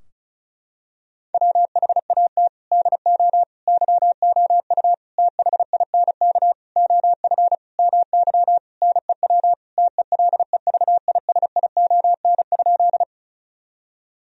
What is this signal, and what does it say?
Morse code, heard as a keyed tone, about 36 wpm, 700 Hz. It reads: WHAT DO YOU THINK OF MY NEW TELEVISION?
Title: Sentences from Top 1000 Words - Review Part 4 - 35wpm